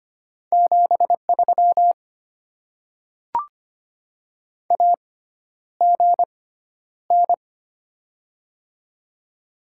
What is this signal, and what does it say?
Morse code, heard as a keyed tone, about 25 wpm, spaced as for 11 wpm, 700 Hz. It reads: 73 AGN
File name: Top 20 QSO Elements - 6x Character Spacing - 25wpm